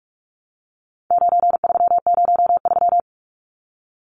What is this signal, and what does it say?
Morse code, heard as a keyed tone, about 44 wpm, 700 Hz. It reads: 9303
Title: Numbers - 4-Digit 45wpm